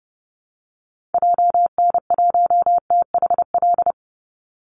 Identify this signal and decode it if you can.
Morse code, heard as a keyed tone, about 30 wpm, 700 Hz. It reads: JN1THL
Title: Call Signs - 30wpm